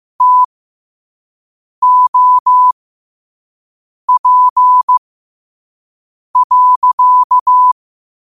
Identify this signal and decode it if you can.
Morse code, heard as a keyed tone, about 15 wpm, spaced as for 5 wpm, 1000 Hz. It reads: TOP.